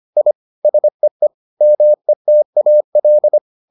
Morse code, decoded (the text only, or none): I SEE METAL